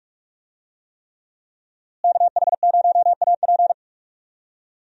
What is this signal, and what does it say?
Morse code, heard as a keyed tone, about 44 wpm, 700 Hz. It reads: KH0AP
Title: USA Call Signs - 5 Characters - Repeated 3x - 45wpm